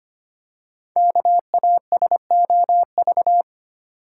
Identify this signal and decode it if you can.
Morse code, heard as a keyed tone, about 25 wpm, 700 Hz. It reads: KASOV
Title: Code Groups - 5 Letters - 25wpm